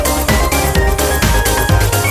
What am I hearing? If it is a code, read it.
There is no code.